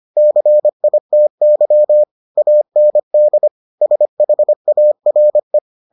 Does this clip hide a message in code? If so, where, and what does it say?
Yes: Morse code from 0.2 to 5.6 s, heard as a keyed tone, about 25 wpm, 600 Hz. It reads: CITY AND SHARE